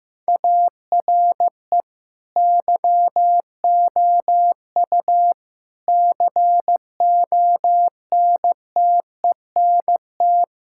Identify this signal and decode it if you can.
Morse code, heard as a keyed tone, about 16 wpm, 700 Hz. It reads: ARE YOU CONTENT
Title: Sentences from Top 1000 Words - Review Part 6 - 15wpm